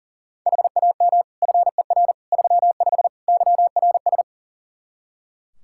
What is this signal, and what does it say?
Morse code, heard as a keyed tone, about 37 wpm, 700 Hz. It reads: HAM FER 35 YRS
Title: Practice QSO - Exchange 3 - More Info - 40wpm